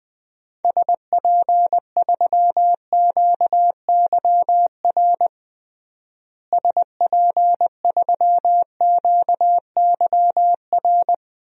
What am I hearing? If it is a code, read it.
Morse code, heard as a keyed tone, about 20 wpm, 700 Hz. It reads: SP3QYR SP3QYR
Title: DXCC-100 Call Signs - 6 Characters - Repeated 3x - 20wpm